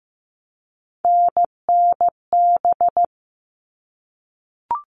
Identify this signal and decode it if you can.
Morse code, heard as a keyed tone, about 15 wpm, 700 Hz. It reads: NNB
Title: All Cut Numbers - 3-Digit 15wpm